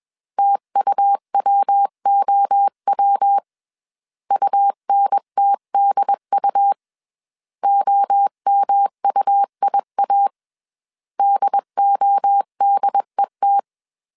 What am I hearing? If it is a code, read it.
Morse code, heard as a keyed tone, about 21 wpm, 800 Hz. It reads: TUWOW UNTDU OMUIA DODET